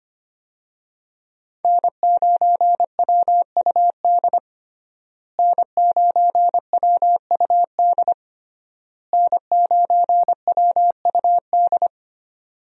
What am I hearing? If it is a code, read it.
Morse code, heard as a keyed tone, about 25 wpm, 700 Hz. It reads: N9WUD N9WUD N9WUD
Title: USA Call Signs - 5 Characters - Repeated 3x - 25wpm